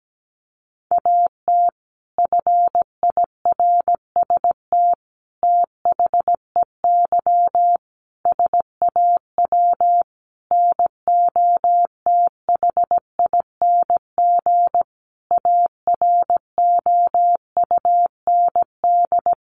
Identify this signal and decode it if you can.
Morse code, heard as a keyed tone, about 17 wpm, 700 Hz. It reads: AT FIRST THEY SAW NOTHING AROUND
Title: Sentences from Top 600 Words 17wpm